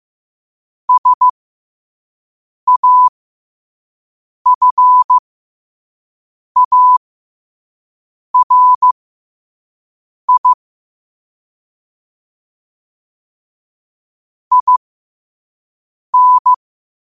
Morse code, heard as a keyed tone, about 15 wpm, spaced as for 5 wpm, 1000 Hz. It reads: SAFARI IN
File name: qotdfiles / 5wpm